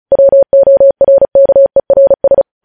Morse code, heard as a keyed tone, about 36 wpm, 550 Hz. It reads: WORKERS
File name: podcast / News headlines-1x-35x35